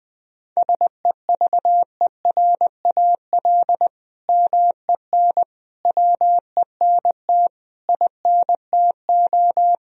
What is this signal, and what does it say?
Morse code, heard as a keyed tone, about 20 wpm, 700 Hz. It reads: SEVERAL MEN WENT INTO
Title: Sentences from Top 1000 Words - Review Part 2 - 20wpm